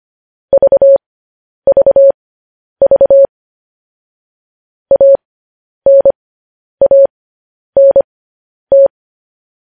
Morse code, heard as a keyed tone, about 25 wpm, spaced as for 10 wpm, 550 Hz. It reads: VVV ANANT
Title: News Headlines at 10